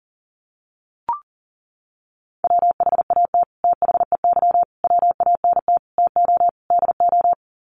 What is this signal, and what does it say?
Morse code, heard as a keyed tone, about 37 wpm, 700 Hz. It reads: WHAT THEY WANT TO DO
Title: Sets of 5 Words Encore2 40wpm